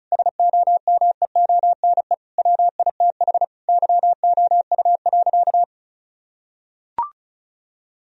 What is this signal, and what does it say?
Morse code, heard as a keyed tone, about 36 wpm, 700 Hz. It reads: SOMEONE WITH YOU.
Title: Sentences from Top 1000 Words - Review Part 4 - 35wpm